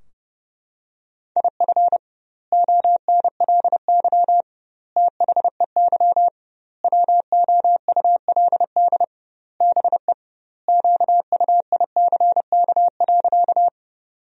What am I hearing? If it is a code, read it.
Morse code, heard as a keyed tone, about 30 wpm, 700 Hz. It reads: IF ONLY THEY WOULD BE QUICK.
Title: 2x Word-Spacing - Sentences from Top 500 Words - 30wpm